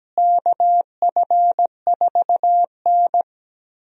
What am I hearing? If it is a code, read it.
Morse code, heard as a keyed tone, about 17 wpm, 700 Hz. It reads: KF4N
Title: USA Call Signs - 4 Characters - Repeated 2x - 17wpm